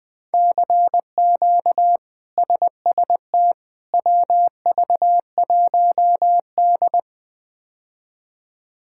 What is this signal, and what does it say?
Morse code, heard as a keyed tone, about 20 wpm, 700 Hz. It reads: CQ SST WV1D
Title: K1USN SST QSO Practice - 20wpm